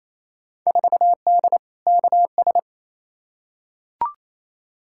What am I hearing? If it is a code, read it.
Morse code, heard as a keyed tone, about 28 wpm, 700 Hz. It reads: 4D KS
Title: Field Day Exchanges Encore - 28wpm